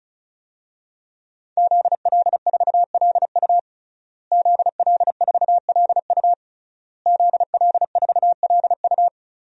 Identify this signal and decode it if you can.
Morse code, heard as a keyed tone, about 36 wpm, 700 Hz. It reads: ZL4LU ZL4LU ZL4LU
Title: DXCC-100 Call Signs - 5 Characters - Repeated 3x - 35wpm